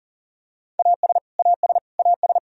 Morse code, heard as a keyed tone, about 40 wpm, 700 Hz. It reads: AS AS AS
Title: Top 100 Words Repeated 40wpm